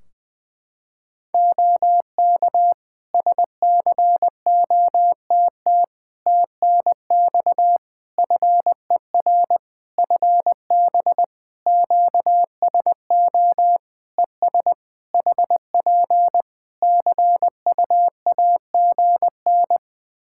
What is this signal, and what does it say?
Morse code, heard as a keyed tone, about 20 wpm, 700 Hz. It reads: OK SCOTT TNX FER FB QSO ES HP CUAGN
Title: Practice QSO - Exchange 4 - Ending - 20wpm